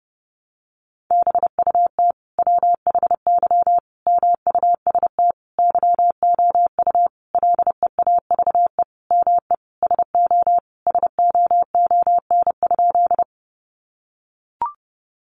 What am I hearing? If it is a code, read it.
Morse code, heard as a keyed tone, about 30 wpm, 700 Hz. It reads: BUT WHY MUST YOU LEAVE ME SO SOON?
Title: Sentences from Top 400 Words 30wpm